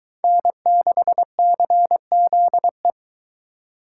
Morse code, heard as a keyed tone, about 22 wpm, 700 Hz. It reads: N6CZE